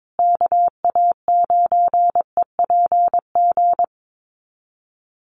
Morse code, heard as a keyed tone, about 22 wpm, 700 Hz. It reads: KA9EPG
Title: USA Call Signs Repeated 3x 22wpm